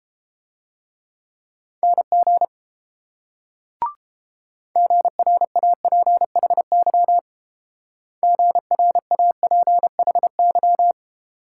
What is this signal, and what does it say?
Morse code, heard as a keyed tone, about 30 wpm, 700 Hz. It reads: NG GRAPHY GRAPHY